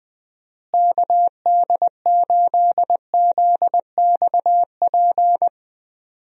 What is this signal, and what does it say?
Morse code, heard as a keyed tone, about 20 wpm, 700 Hz. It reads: KD8ZXP
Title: USA Call Signs - 6 Characters - Repeated 3x - 20wpm